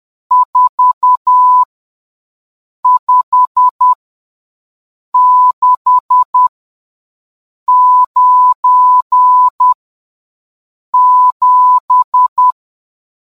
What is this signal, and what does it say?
Morse code, heard as a keyed tone, about 10 wpm, spaced as for 5 wpm, 1000 Hz. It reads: 45697